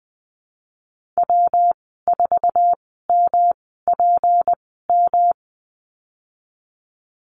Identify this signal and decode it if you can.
Morse code, heard as a keyed tone, about 20 wpm, spaced as for 15 wpm, 700 Hz. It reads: W4MPM
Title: 186 Lesson 30g - Letter b - Call Signs 20wpm